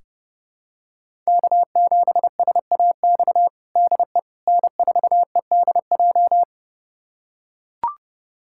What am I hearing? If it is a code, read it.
Morse code, heard as a keyed tone, about 32 wpm, 700 Hz. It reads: K7SAX DE N4EDJ
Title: USA Callsign DE Callsign - 5 Characters - 30wpm